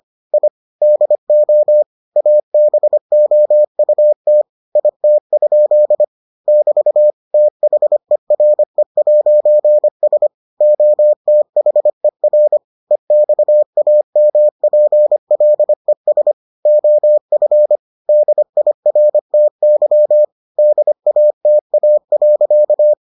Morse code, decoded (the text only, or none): I DO ABOUT IT? <BT> THERE'S OTHER EXAMPLES OF DIRTY DATA.